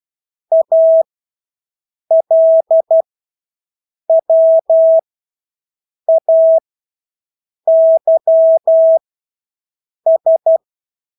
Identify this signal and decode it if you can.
Morse code, heard as a keyed tone, about 12 wpm, spaced as for 6 wpm, 650 Hz. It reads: ALWAYS
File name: Engelske ord 12-6 wpm